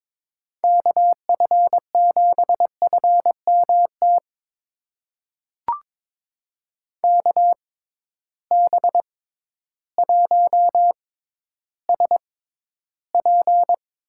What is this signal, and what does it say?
Morse code, heard as a keyed tone, about 22 wpm, spaced as for 12 wpm, 700 Hz. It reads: KF7FMT KB1SP
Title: USA Call Signs - 6 Characters - 6x Character Spacing - 22wpm